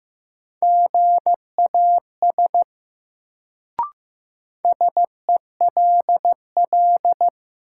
Morse code, heard as a keyed tone, about 15 wpm, 700 Hz. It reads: GAS SELL
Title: Sentences from Top 1100 Words - New Word Drill Alt - 15wpm